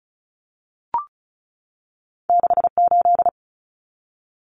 Morse code, heard as a keyed tone, about 34 wpm, 700 Hz. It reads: 68
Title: Numbers - 2-Digit 35wpm